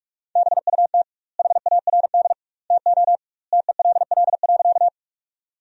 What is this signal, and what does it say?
Morse code, heard as a keyed tone, about 46 wpm, 700 Hz. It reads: BUT HARD TO TELL.
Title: 2x Word-Spacing - Sentences from Top 300 Words - 45wpm